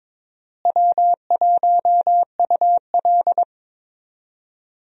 Morse code, heard as a keyed tone, about 22 wpm, 700 Hz. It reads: W1UL
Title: USA Call Signs - 4 Characters - Repeated 3x - 22wpm